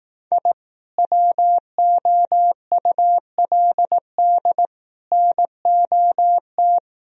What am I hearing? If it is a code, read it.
Morse code, heard as a keyed tone, about 18 wpm, 700 Hz. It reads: I WOULD NOT
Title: Speed-Racing - Sets of 5 Words 15wpm